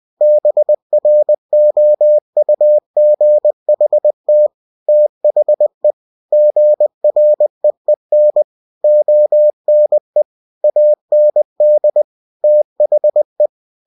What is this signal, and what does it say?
Morse code, heard as a keyed tone, about 20 wpm, 600 Hz. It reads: BROUGHT THE GREEN ONE AND THE
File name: cw-books / winnie-the-pooh / winnie-20wpm-page-0005